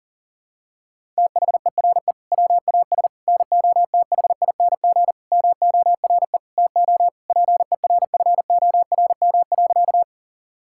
Morse code, heard as a keyed tone, about 40 wpm, 700 Hz. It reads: THERE WAS NOTHING MORE TO PERFORM.
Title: Sentences from Top 1000 Words - Review Part 5 - 40wpm